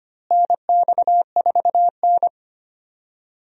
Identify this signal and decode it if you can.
Morse code, heard as a keyed tone, about 25 wpm, 700 Hz. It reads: NX4N
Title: USA Call Signs - 4 Characters - Repeated 3x - 25wpm